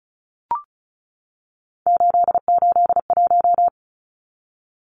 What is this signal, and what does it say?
Morse code, heard as a keyed tone, about 34 wpm, 700 Hz. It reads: 881
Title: Numbers - 3-Digit 35wpm